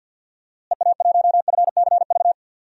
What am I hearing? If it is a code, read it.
Morse code, heard as a keyed tone, about 52 wpm, 700 Hz. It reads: EA1FCV
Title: DXCC-100 Call Signs - 6 Characters - Repeated 2x - 50wpm